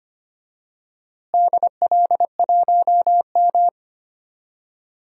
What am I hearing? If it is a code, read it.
Morse code, heard as a keyed tone, about 25 wpm, 700 Hz. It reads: DL1M
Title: DXCC-100 Call Signs - 4 Characters - Repeated 2x - 25wpm